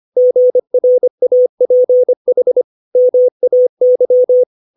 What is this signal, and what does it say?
Morse code, heard as a keyed tone, about 25 wpm, 500 Hz. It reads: GRAPH MAY
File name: 632 words / 632 Words 25WPM